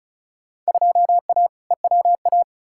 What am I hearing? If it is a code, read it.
Morse code, heard as a keyed tone, about 34 wpm, 700 Hz. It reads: 2A EWA
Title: Field Day Exchanges - 35wpm